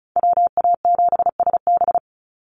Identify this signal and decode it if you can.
Morse code, heard as a keyed tone, about 34 wpm, 700 Hz. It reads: WA7SB
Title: NA SOTA Chaser Callsigns 35wpm